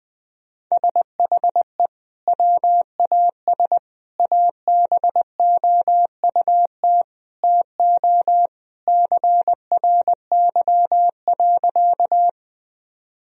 Morse code, heard as a keyed tone, about 20 wpm, 700 Hz. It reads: SHE WAS ABOUT TO CRY.
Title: Sentences from Top 1000 Words - Review Part 2 - 20wpm